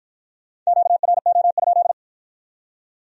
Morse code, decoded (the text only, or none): QRO?